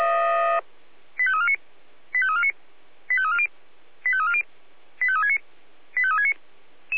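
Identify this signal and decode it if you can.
Five-tone selective call (ZVEI), heard as quick runs of five tones, tones of about 70 ms. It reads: 85349 85349 85340 85340 85369 85369
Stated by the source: ZVEI five-tone call 85355 received on an SDR.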